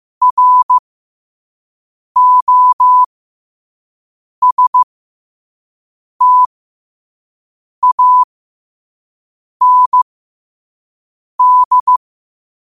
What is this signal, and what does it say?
Morse code, heard as a keyed tone, about 15 wpm, spaced as for 5 wpm, 1000 Hz. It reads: ROSTAND
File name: qotdfiles / 5wpm